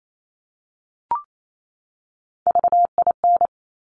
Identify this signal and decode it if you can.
Morse code, heard as a keyed tone, about 28 wpm, 700 Hz. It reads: VIN
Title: Top Three Letter Combinations in Words 28wpm